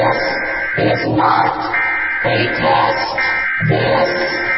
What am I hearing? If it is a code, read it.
Emergency Alert System (SAME) tones, heard as harsh buzzing bursts. It is heard from 0.3 to 3.6 s.